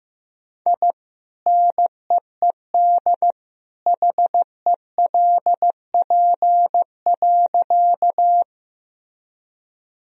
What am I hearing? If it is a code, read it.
Morse code, heard as a keyed tone, about 15 wpm, 700 Hz. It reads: I NEED HELP.
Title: Sentences from Top 1000 Words - Review Part 1 - 15wpm